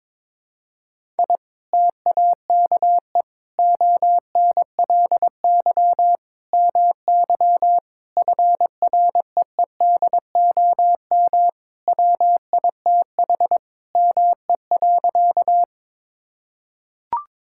Morse code, heard as a keyed tone, about 22 wpm, 700 Hz. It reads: I TAKE ONLY MY FREEDOM WITH ME.
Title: Sentences from Top 1000 Words - Review Part 5 - 22wpm